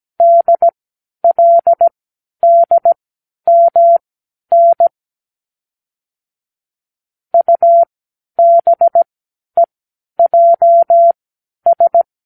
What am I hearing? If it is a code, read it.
Morse code, heard as a keyed tone, about 17 wpm, spaced as for 10 wpm, 680 Hz. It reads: DLDMN UBEJS
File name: k7qo / Lesson 36 (A-X)